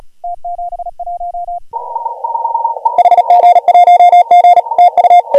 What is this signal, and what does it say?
Morse code, heard as a keyed tone, about 38 wpm, 720 Hz. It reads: T71 HG1G TU
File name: AUDIO2019 / HG1G CW